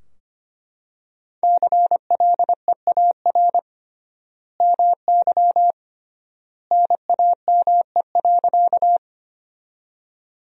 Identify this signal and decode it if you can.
Morse code, heard as a keyed tone, about 25 wpm, 700 Hz. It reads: CLEAR MY NAME.
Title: 3x Word-Spacing - Sentences from Top 500 Words - 25wpm